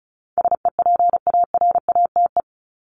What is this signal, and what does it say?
Morse code, heard as a keyed tone, about 34 wpm, 700 Hz. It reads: SEPARATE
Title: Top 751-1000 Words 35wpm